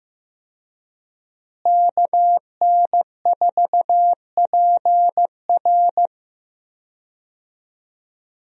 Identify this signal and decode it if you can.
Morse code, heard as a keyed tone, about 15 wpm, 700 Hz. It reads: KN4PR
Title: USA Call Signs - 5 Characters - Repeated 2x - 15wpm